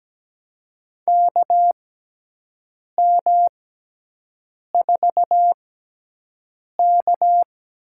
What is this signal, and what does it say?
Morse code, heard as a keyed tone, about 17 wpm, spaced as for 6 wpm, 700 Hz. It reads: KM4K
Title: USA Call Signs - 6 Characters - 6x Character Spacing - 17wpm